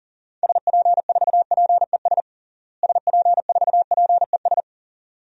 Morse code, heard as a keyed tone, about 40 wpm, 700 Hz. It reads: SP4PES SP4PES
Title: DXCC-100 Call Signs - 6 Characters - Repeated 2x - 40wpm